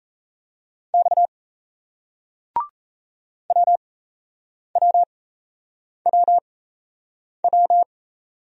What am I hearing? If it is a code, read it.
Morse code, heard as a keyed tone, about 38 wpm, spaced as for 8 wpm, 700 Hz. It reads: X WWWW